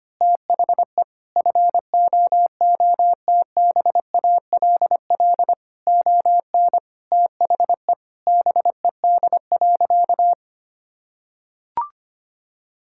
Morse code, heard as a keyed tone, about 25 wpm, 700 Hz. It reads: THE FOOTBALL ON THE BED.
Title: Sentences from Top 1600 Words 25wpm